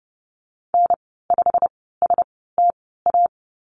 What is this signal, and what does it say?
Morse code, heard as a keyed tone, about 30 wpm, spaced as for 17 wpm, 700 Hz. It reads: N5STA
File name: Lesson 15g - Number 5 - Call Signs 30wpm